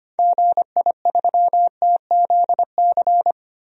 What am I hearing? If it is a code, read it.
Morse code, heard as a keyed tone, about 25 wpm, 700 Hz. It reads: GI3TZC
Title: DXCC-100 Call Signs - 6 Characters - 25wpm